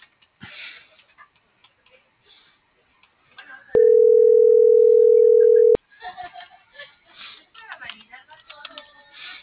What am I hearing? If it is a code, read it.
Ringback tone (North America), 440 + 480 Hz.
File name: CENTRAL MABE / out xquenda.trinidad - .62352